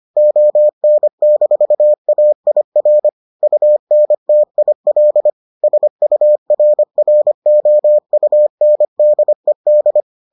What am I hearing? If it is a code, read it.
Morse code, heard as a keyed tone, about 25 wpm, 600 Hz. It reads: ON-AIR UNTIL SURROUNDED